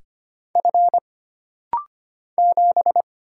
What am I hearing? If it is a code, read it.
Morse code, heard as a keyed tone, about 25 wpm, 700 Hz. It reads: F 7
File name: Single Character (Letter:Number:Punctuation:Prosign) - Rapid-Fire - 25wpm